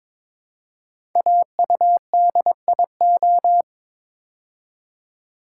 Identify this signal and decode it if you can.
Morse code, heard as a keyed tone, about 22 wpm, 700 Hz. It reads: AUDIO
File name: 5 Letter Words - First 3 Letters Given - 22wpm